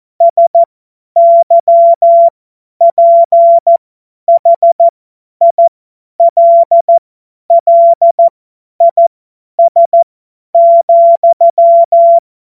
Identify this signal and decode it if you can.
Morse code, heard as a keyed tone, about 14 wpm, spaced as for 10 wpm, 680 Hz. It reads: SYPHILLIS,
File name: Morsekurs / Witze / Witz 26 März 2024 50 F70